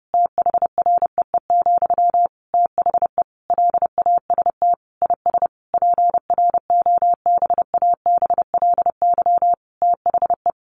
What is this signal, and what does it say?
Morse code, heard as a keyed tone, about 28 wpm, 700 Hz. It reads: THREE, THE LAST IS PROBABLY THE
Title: Sentences from Top 700 Words 30wpm